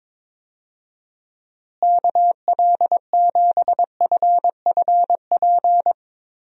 Morse code, heard as a keyed tone, about 22 wpm, 700 Hz. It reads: KL7FFP
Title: USA Call Signs - 6 Characters - Repeated 3x - 22wpm